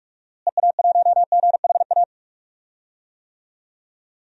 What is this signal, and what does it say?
Morse code, heard as a keyed tone, about 46 wpm, 700 Hz. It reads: EA1GHA
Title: DXCC-100 Call Signs - 6 Characters - Repeated 3x - 45wpm